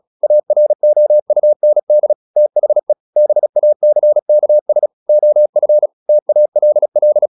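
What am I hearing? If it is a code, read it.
Morse code, heard as a keyed tone, about 36 wpm, 600 Hz. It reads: AROUND THE BACKS OF TALL
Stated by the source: Morse practice at 36 wpm, text of Ken Follett's novel Never, from CW med Gnister.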